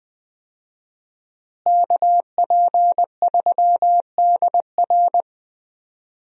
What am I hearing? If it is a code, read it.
Morse code, heard as a keyed tone, about 20 wpm, 700 Hz. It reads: KP3DR
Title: USA Call Signs - 5 Characters - Repeated 2x - 20wpm